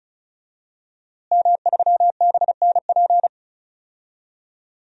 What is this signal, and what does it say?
Morse code, heard as a keyed tone, about 34 wpm, 700 Hz. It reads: M3BNP
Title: DXCC-100 Call Signs - 5 Characters - Repeated 3x - 35wpm